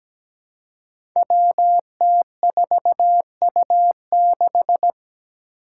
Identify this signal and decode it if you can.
Morse code, heard as a keyed tone, about 17 wpm, 700 Hz. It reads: WT4U6
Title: Five Challenging Letter-Number Combinations 17wpm